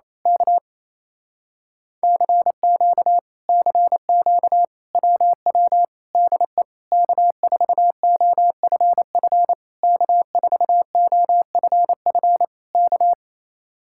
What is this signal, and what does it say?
Morse code, heard as a keyed tone, about 26 wpm, 700 Hz. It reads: K CQ CQ WW DE K4OFF K4OFF K
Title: Speed-Racing Common QSO Phrases Encore - 28wpm